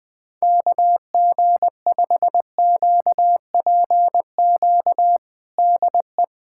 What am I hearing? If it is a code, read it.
Morse code, heard as a keyed tone, about 20 wpm, 700 Hz. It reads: KG5QPQ DE
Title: USA Callsign DE Callsign - 6 Characters - 20wpm